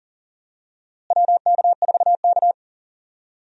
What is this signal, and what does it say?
Morse code, heard as a keyed tone, about 40 wpm, 700 Hz. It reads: WK4K
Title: USA Call Signs - 4 Characters - Repeated 3x - 40wpm